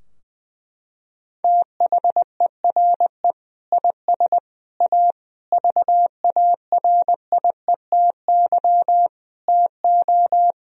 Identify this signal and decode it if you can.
Morse code, heard as a keyed tone, about 19 wpm, 700 Hz. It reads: THERE IS A VARIETY TO
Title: Sentences from Top 1000 Words - Review Part 4 - 20wpm